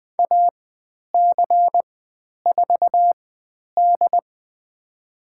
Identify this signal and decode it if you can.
Morse code, heard as a keyed tone, about 20 wpm, spaced as for 10 wpm, 700 Hz. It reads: AC4D
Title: 109 Lesson 18g - Letter m - Call Signs 20wpm